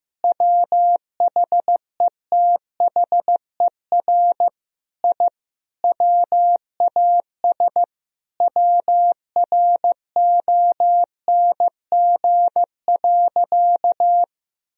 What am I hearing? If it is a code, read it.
Morse code, heard as a keyed tone, about 15 wpm, 700 Hz. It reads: WHETHER I WAS WRONG.
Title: Sentences from Top 1000 Words - Review Part 3 - 15wpm